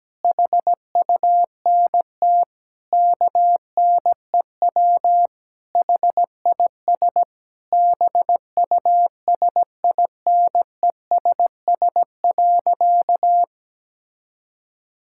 Morse code, heard as a keyed tone, about 17 wpm, 700 Hz. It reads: HUNT KNEW HIS BUSINESS.
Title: Sentences from Top 1600 Words 17wpm